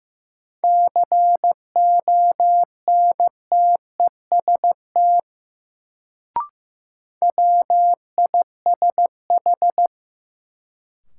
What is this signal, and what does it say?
Morse code, heard as a keyed tone, about 15 wpm, 700 Hz. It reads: CONTEST WISH
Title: Sentences from Top 1600 Words - New Word Drill Alt - 15wpm